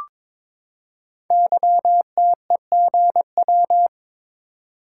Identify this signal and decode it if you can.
Morse code, heard as a keyed tone, about 23 wpm, 700 Hz. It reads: YTEGW
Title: Code Groups - 5 Letter:Numbers - 22wpm